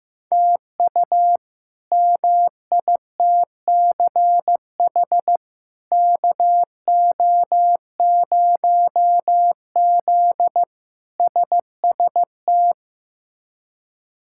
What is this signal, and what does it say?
Morse code, heard as a keyed tone, about 15 wpm, 700 Hz. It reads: TU MITCH KO0Z SST
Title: K1USN SST QSO Practice - 15wpm